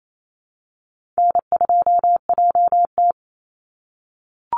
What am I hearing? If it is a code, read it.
Morse code, heard as a keyed tone, about 28 wpm, 700 Hz. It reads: N2JT
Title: Easy Call Signs - 28wpm